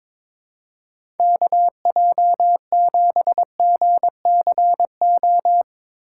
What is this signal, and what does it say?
Morse code, heard as a keyed tone, about 22 wpm, 700 Hz. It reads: KJ7GCO
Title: USA Call Signs - 6 Characters - 3x Character Spacing - 22wpm